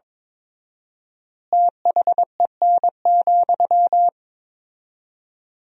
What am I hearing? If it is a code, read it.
Morse code, heard as a keyed tone, about 22 wpm, 700 Hz. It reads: THEN,
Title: 4x Word-Spacing - Sentences from Top 100 Words - 22wpm